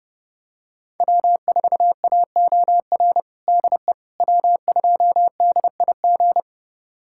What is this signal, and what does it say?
Morse code, heard as a keyed tone, about 32 wpm, 700 Hz. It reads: W4AOR DE W2DIG
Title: USA Callsign DE Callsign - 5 Characters - 30wpm